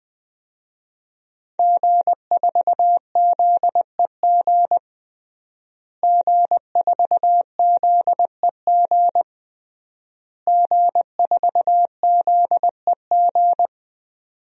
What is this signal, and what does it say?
Morse code, heard as a keyed tone, about 20 wpm, 700 Hz. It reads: G4ZEG G4ZEG G4ZEG
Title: DXCC-100 Call Signs - 5 Characters - Repeated 3x - 20wpm